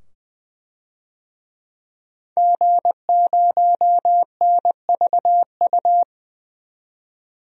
Morse code, heard as a keyed tone, about 20 wpm, 700 Hz. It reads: G0NVU
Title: DXCC-100 Call Signs - 5 Characters - 20wpm